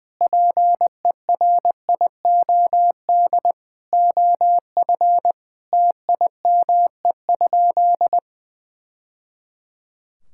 Morse code, heard as a keyed tone, about 21 wpm, 700 Hz. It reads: PERIOD OF TIME?
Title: Sentences from Top 1000 Words - Review Part 5 - 20wpm